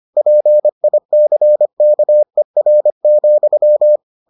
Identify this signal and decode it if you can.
Morse code, heard as a keyed tone, about 25 wpm, 600 Hz. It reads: PICKER,